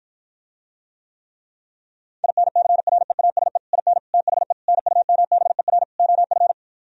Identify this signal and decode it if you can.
Morse code, heard as a keyed tone, about 53 wpm, 700 Hz. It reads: INCREASE IN THE NUMBER OF